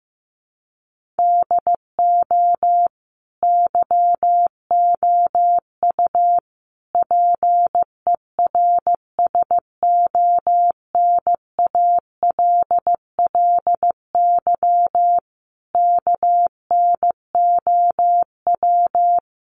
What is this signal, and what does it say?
Morse code, heard as a keyed tone, about 15 wpm, 700 Hz. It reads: DO YOU PERSONALLY KNOW
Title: Sentences from Top 1400 Words 15wpm